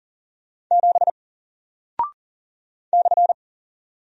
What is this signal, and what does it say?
Morse code, heard as a keyed tone, about 40 wpm, 700 Hz. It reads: 7 /
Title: Single Character (Letter:Number:Punctuation:Prosign) Alt 40wpm